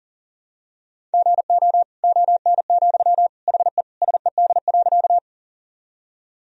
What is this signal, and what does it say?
Morse code, heard as a keyed tone, about 40 wpm, 700 Hz. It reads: GO ON, HE SED.
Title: Sentences from Top 100 Words and Abbreviations 40wpm